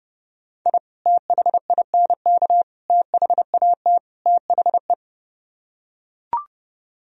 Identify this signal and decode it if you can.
Morse code, heard as a keyed tone, about 29 wpm, 700 Hz. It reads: I THINK THAT THE